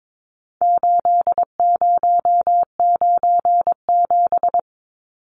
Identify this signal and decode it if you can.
Morse code, heard as a keyed tone, about 22 wpm, 700 Hz. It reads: 8097
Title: Numbers - 4-Digit 22wpm